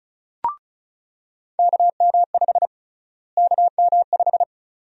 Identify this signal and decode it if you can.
Morse code, heard as a keyed tone, about 36 wpm, 700 Hz. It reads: KM5 KM5
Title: USA Call Sign Prefixes - Repeated 2x - 35wpm